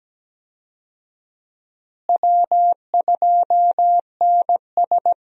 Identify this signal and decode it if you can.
Morse code, heard as a keyed tone, about 17 wpm, 700 Hz. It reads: W2NS
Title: USA Call Signs - 4 Characters - Repeated 2x - 17wpm